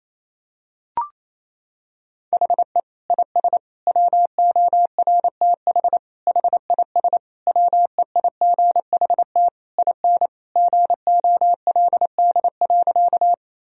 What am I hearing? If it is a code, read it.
Morse code, heard as a keyed tone, about 28 wpm, 700 Hz. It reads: HE IS WORTH HIS WEIGHT IN GOLD.
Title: Sentences from Top 1000 Words - Review Part 3 - 28wpm